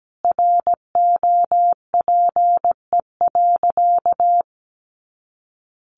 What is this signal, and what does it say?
Morse code, heard as a keyed tone, about 17 wpm, 700 Hz. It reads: ROPE.
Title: Easy Phrases 17wpm